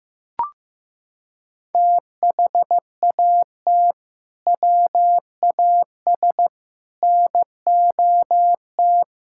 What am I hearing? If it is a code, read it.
Morse code, heard as a keyed tone, about 15 wpm, 700 Hz. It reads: THAT WAS NOT
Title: Sentences from Top 1000 Words - Review Part 5 - 15wpm